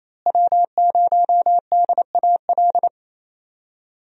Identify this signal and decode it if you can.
Morse code, heard as a keyed tone, about 28 wpm, 700 Hz. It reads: W0DAL
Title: USA Call Signs - 5 Characters - Repeated 3x - 28wpm